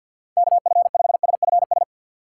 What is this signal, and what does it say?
Morse code, heard as a keyed tone, about 49 wpm, 700 Hz. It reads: KF5SFS
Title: Speed Racing - USA 2x3 Call Signs 50wpm